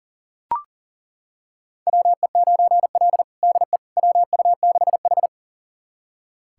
Morse code, heard as a keyed tone, about 37 wpm, 700 Hz. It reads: WE9L DE WU6H
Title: USA Callsign DE Callsign - 4 Characters - 40wpm